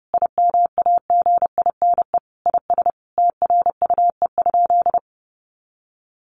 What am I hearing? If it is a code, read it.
Morse code, heard as a keyed tone, about 30 wpm, 700 Hz. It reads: IMAGINE IS TRUE?
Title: Sentences from Top 1400 Words 30wpm